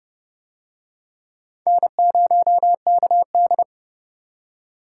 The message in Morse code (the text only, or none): N0KD